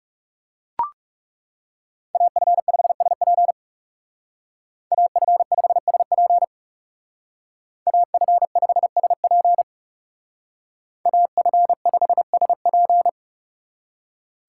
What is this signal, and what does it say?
Morse code, heard as a keyed tone, about 39 wpm, 700 Hz. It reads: AF5SP AF5SP AF5SP AF5SP